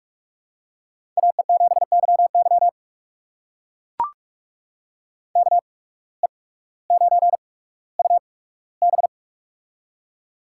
Morse code, heard as a keyed tone, about 44 wpm, spaced as for 19 wpm, 700 Hz. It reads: AE7YY KE9UB
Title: USA Call Signs - 5 Characters - 8x Character Spacing - 45wpm